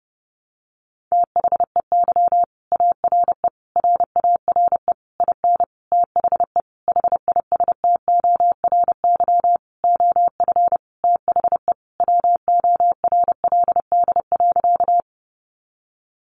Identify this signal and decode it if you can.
Morse code, heard as a keyed tone, about 30 wpm, 700 Hz. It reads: THEY ARE RARE IN THE HISTORY OF THE WORLD.
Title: Sentences from Top 1100 Words 30wpm